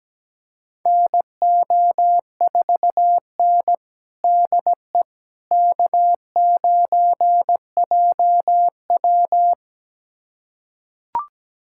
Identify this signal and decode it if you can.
Morse code, heard as a keyed tone, about 17 wpm, 700 Hz. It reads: NO4N DE K9JW
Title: USA Callsign DE Callsign - 4 Characters - 17wpm